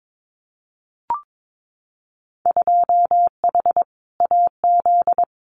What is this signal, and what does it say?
Morse code, heard as a keyed tone, about 22 wpm, 700 Hz. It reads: 2H AZ
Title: Winter Field Day Exchanges 22wpm